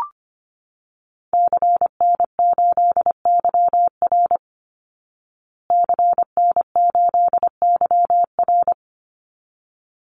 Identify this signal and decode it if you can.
Morse code, heard as a keyed tone, about 25 wpm, 700 Hz. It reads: CN8YR CN8YR
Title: Call Signs Repeated 3x - 25wpm